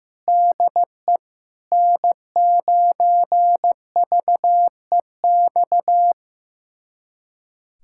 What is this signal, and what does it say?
Morse code, heard as a keyed tone, about 15 wpm, 700 Hz. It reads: DE N9VEX
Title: USA Callsign DE Callsign - 5 Characters - 15wpm